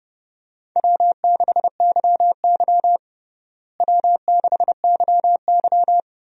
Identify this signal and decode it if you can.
Morse code, heard as a keyed tone, about 31 wpm, 700 Hz. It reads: W6YY W6YY
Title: USA Call Signs - 4 Characters - Repeated 2x - 30wpm